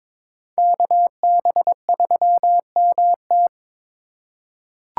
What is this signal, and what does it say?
Morse code, heard as a keyed tone, about 21 wpm, 700 Hz. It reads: KB3MT
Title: USA Call Signs - 3x Character Spacing - 22wpm